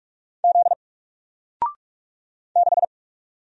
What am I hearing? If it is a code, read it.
Morse code, heard as a keyed tone, about 44 wpm, 700 Hz. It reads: Z 6
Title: Single Letter-Number Alt 45wpm